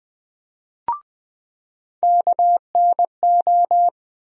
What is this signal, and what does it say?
Morse code, heard as a keyed tone, about 20 wpm, 700 Hz. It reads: KNO
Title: USA Call Sign Postfixes - 3 Letters 20wpm